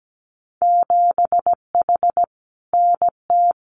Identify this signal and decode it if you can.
Morse code, heard as a keyed tone, about 17 wpm, 700 Hz. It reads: 7H NT
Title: Winter Field Day Exchanges Encore 17wpm